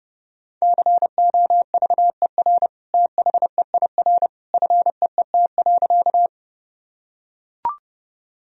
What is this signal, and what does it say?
Morse code, heard as a keyed tone, about 30 wpm, 700 Hz. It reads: COVER THEIR FEET.
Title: Sentences from Top 1000 Words - Review Part 1 - 30wpm